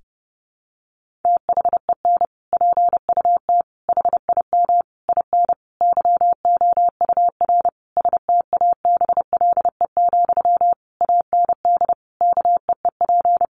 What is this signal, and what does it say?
Morse code, heard as a keyed tone, about 30 wpm, 700 Hz. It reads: THEN PUT HIM IN YOUR STABLE, AND KEEP
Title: Sentences from Top 1500 Words 30wpm